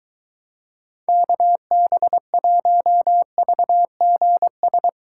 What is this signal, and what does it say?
Morse code, heard as a keyed tone, about 23 wpm, 700 Hz. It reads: KB1VGS